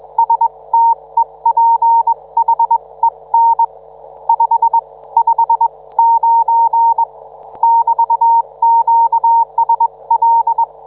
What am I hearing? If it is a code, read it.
Morse code, heard as a keyed tone, about 21 wpm, spaced as for 18 wpm, 910 Hz. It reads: STEPHEN 5 59 <BT>QSL